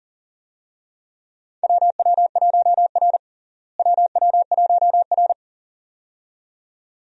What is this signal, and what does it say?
Morse code, heard as a keyed tone, about 40 wpm, 700 Hz. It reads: WW1R WW1R
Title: USA Call Signs - 4 Characters - Repeated 2x - 40wpm